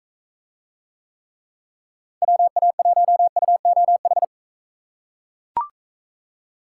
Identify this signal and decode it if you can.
Morse code, heard as a keyed tone, about 40 wpm, 700 Hz. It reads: WA1UOH